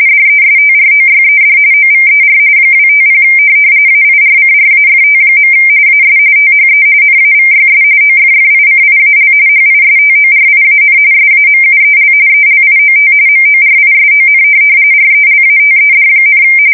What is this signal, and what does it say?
RTTY, about 45.45 baud. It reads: DESTROYED OVER 250 AIRCRAFT. AS THE NAVYS OFFICIAL HISTORIAN SAMUEL ELIOT MORISON RELATED,
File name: fsk170 / USS-New-Jersey